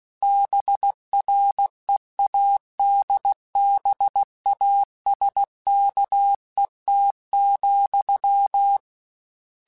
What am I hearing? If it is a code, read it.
Morse code, heard as a keyed tone, about 16 wpm, 800 Hz. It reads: BREADBASKET,
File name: Misc / Misc-mp3s / BIG @15